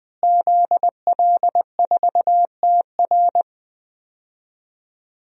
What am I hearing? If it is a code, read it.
Morse code, heard as a keyed tone, about 20 wpm, 700 Hz. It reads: ZL4TR
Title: DXCC-100 Call Signs - 5 Characters - Repeated 2x - 20wpm